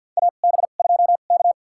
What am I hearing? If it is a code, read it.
Morse code, heard as a keyed tone, about 53 wpm, spaced as for 37 wpm, 700 Hz. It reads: AB2K
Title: USA Call Signs - 4 Characters - 2x Character Spacing - 50wpm